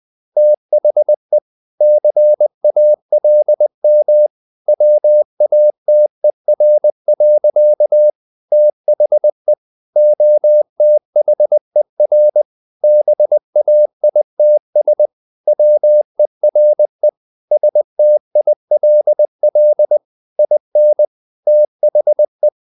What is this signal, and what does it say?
Morse code, heard as a keyed tone, about 20 wpm, 600 Hz. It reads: THE CALM WATER. THE OTHER BAITS WERE STILL IN THE
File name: the old man and the sea / the old man and the sea part 1 20wpm